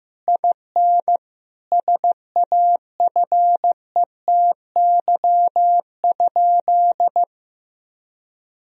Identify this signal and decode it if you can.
Morse code, heard as a keyed tone, about 16 wpm, 700 Hz. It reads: IN SAFETY?
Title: Sentences from Top 1000 Words - Review Part 4 - 15wpm